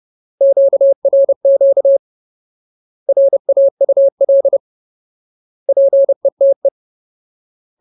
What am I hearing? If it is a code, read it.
Morse code, heard as a keyed tone, about 30 wpm, 550 Hz. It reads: QRQ RAUL PETE
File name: Ultimate Word List / 30WPM Ultimate